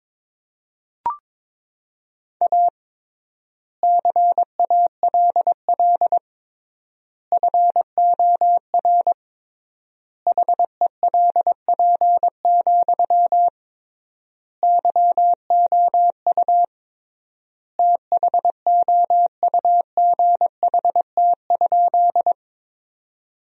Morse code, heard as a keyed tone, about 22 wpm, 700 Hz. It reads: A CALL FOR HELP, YOU THOUGHT?
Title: 3x Word-Spacing - Sentences from Top 300 Words - 22wpm